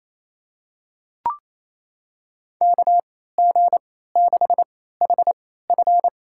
Morse code, heard as a keyed tone, about 27 wpm, spaced as for 16 wpm, 700 Hz. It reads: KG6HF
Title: USA Call Signs - 3x Character Spacing - 28wpm